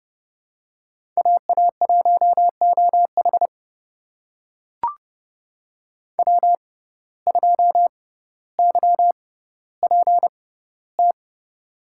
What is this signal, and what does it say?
Morse code, heard as a keyed tone, about 30 wpm, spaced as for 15 wpm, 700 Hz. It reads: AA1OH W2YPT
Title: USA Call Signs - 5 Characters - 6x Character Spacing - 30wpm